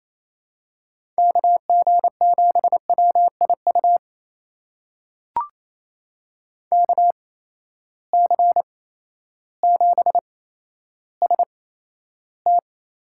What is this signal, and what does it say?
Morse code, heard as a keyed tone, about 28 wpm, spaced as for 13 wpm, 700 Hz. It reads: KG7WIU KC7ST
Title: USA Call Signs - 6 Characters - 8x Character Spacing - 28wpm